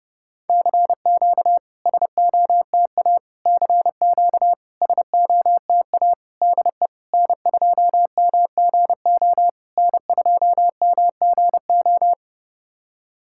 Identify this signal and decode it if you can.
Morse code, heard as a keyed tone, about 30 wpm, 690 Hz. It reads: CQ SOTA CQ SOTA DE N2MGO N2MGO
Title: SOTA QSO Practice 30wpm